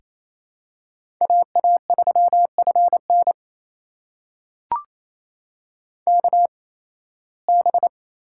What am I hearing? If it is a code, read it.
Morse code, heard as a keyed tone, about 27 wpm, spaced as for 18 wpm, 700 Hz. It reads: AA3FN KB
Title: USA Call Signs - 8x Character Spacing - 28wpm